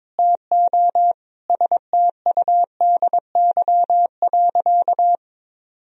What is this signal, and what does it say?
Morse code, heard as a keyed tone, about 21 wpm, 700 Hz. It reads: TO STUDY.
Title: Sentences from Top 1000 Words - Review Part 1 - 22wpm